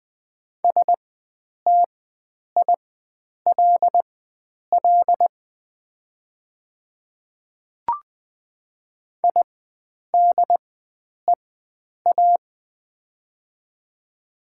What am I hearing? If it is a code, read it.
Morse code, heard as a keyed tone, about 20 wpm, spaced as for 9 wpm, 700 Hz. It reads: STILL IDEA
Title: 4x Character Spacing - Top 300 Words - 20wpm